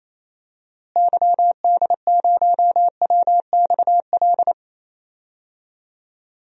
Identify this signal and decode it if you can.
Morse code, heard as a keyed tone, about 28 wpm, 700 Hz. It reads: YD0WXL
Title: DXCC-100 Call Signs - 6 Characters - Repeated 2x - 28wpm